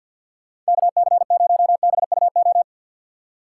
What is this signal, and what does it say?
Morse code, heard as a keyed tone, about 60 wpm, 700 Hz. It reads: KC0BUO
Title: Speed Racing - USA 2x3 Call Signs 50wpm